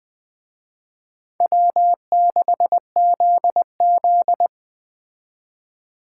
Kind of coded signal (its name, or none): Morse code